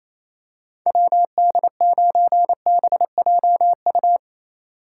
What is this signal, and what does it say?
Morse code, heard as a keyed tone, about 28 wpm, 700 Hz. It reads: WD9BJU
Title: USA Call Signs - 6 Characters - 2x Character Spacing - 28wpm